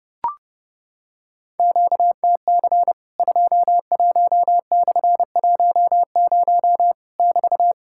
Morse code, heard as a keyed tone, about 30 wpm, 700 Hz. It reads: QTC 21/10 <BT>
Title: Worked All Europe DX Contest - QTC Exchange - 30wpm